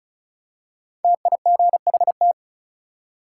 Morse code, heard as a keyed tone, about 36 wpm, 700 Hz. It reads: TIGHT
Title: Five Letter Words - 35wpm